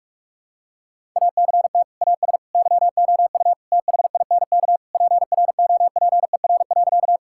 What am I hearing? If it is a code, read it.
Morse code, heard as a keyed tone, about 46 wpm, 700 Hz. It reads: ACT AS YOU THINK PROPER.
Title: Sentences from Top 1000 Words - Review Part 5 - 45wpm